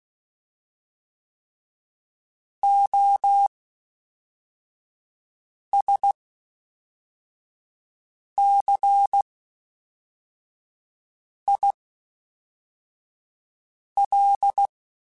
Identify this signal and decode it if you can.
Morse code, heard as a keyed tone, about 16 wpm, spaced as for 4 wpm, 800 Hz. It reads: OSCIL
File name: Special INGs / Special INGs-mp3s / Fast ing @10